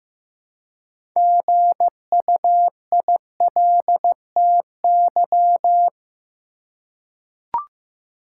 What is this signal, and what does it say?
Morse code, heard as a keyed tone, about 15 wpm, 700 Hz. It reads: GUILTY
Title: Six Letter Words - 15wpm